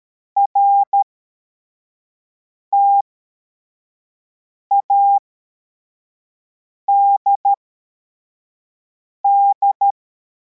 Morse code, heard as a keyed tone, about 13 wpm, spaced as for 4 wpm, 800 Hz. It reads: RTADD
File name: formation / cw / Text21